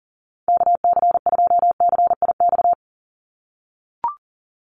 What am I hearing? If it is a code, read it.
Morse code, heard as a keyed tone, about 40 wpm, 700 Hz. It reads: KC2CIX
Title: USA Call Signs 40wpm